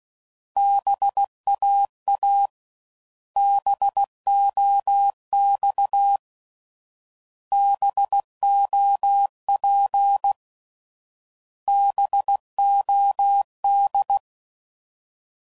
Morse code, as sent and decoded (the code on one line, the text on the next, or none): -... .- .- / -... --- -..- / -... --- .--. / -... --- -..
BAA BOX BOP BOD